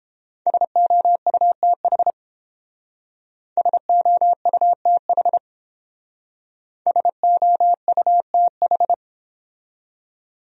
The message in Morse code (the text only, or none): SOUTH SOUTH SOUTH